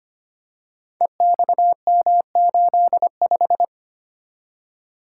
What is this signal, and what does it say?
Morse code, heard as a keyed tone, about 25 wpm, 700 Hz. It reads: EXM85
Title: Code Groups - 5 Letter:Numbers - 25wpm